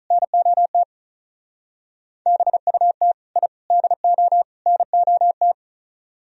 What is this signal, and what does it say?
Morse code, heard as a keyed tone, about 44 wpm, 700 Hz. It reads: NOT BUT I DO NOT